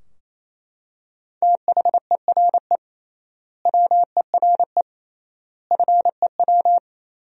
Morse code, heard as a keyed tone, about 27 wpm, 700 Hz. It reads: THERE WERE FEW
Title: 3x Word-Spacing - Sentences from Top 300 Words - 28wpm